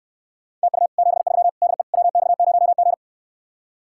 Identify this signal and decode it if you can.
Morse code, heard as a keyed tone, about 69 wpm, 700 Hz. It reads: TU 73 DE WX0G